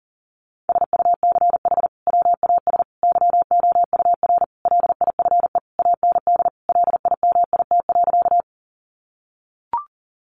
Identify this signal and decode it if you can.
Morse code, heard as a keyed tone, about 40 wpm, 700 Hz. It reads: SUCH WAS YOUR LIFE AND LIMIT.
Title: Sentences from Top 1500 Words 40wpm